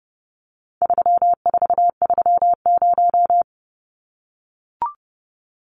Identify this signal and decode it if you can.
Morse code, heard as a keyed tone, about 30 wpm, 700 Hz. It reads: 3430
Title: Numbers - 4-Digit 30wpm